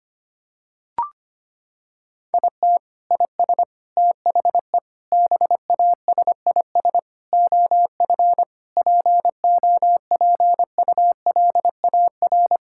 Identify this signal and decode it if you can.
Morse code, heard as a keyed tone, about 25 wpm, 700 Hz. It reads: IT IS THE BASIS OF POPULAR
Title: Sentences from Top 1000 Words - Review Part 5 - 25wpm